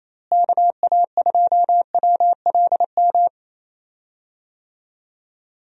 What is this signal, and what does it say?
Morse code, heard as a keyed tone, about 28 wpm, 700 Hz. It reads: KA2WLM
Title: USA Call Signs - 6 Characters - Repeated 3x - 28wpm